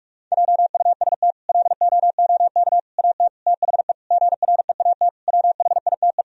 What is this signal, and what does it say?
Morse code, heard as a keyed tone, about 46 wpm, 700 Hz. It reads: JUST LOOK AT THE GREAT WHITE
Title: Sentences from Top 1000 Words - Review Part 1 - 45wpm